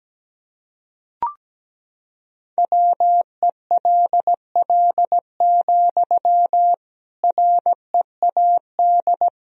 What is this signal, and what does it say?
Morse code, heard as a keyed tone, about 17 wpm, 700 Hz. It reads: WELL, READ
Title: Sentences from Top 300 Words and Abbreviations 17wpm